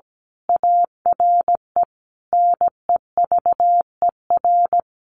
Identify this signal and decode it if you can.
Morse code, heard as a keyed tone, about 17 wpm, 700 Hz. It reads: ARE NEVER
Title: A Sound Of Thunder 17wpm